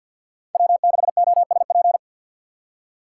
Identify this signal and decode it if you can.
Morse code, heard as a keyed tone, about 48 wpm, 700 Hz. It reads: W6OSP
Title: K1USN SST QSO Practice - 50wpm